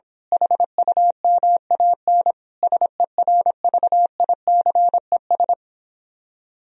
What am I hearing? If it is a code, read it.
Morse code, heard as a keyed tone, about 26 wpm, 700 Hz. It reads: HUMAN SERVICES
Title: Speed-Racing - Sets of 5 Words 22wpm